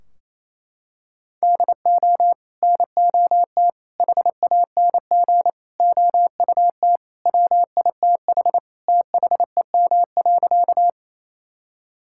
Morse code, heard as a keyed tone, about 27 wpm, 700 Hz. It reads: DO NOT HANG OUT WITH THEM.
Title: Sentences from Top 1600 Words 28wpm